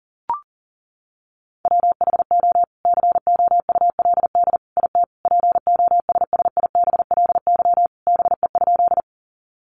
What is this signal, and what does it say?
Morse code, heard as a keyed tone, about 40 wpm, 700 Hz. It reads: WHO COULD IT POSSIBLY BE?
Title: Sentences from Top 1300 Words 40wpm